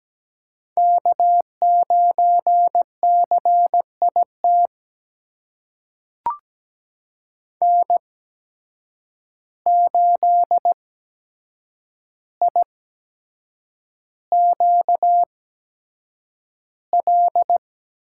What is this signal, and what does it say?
Morse code, heard as a keyed tone, about 17 wpm, spaced as for 7 wpm, 700 Hz. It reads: K9CIT N8IQL
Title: USA Call Signs - 5 Characters - 8x Character Spacing - 17wpm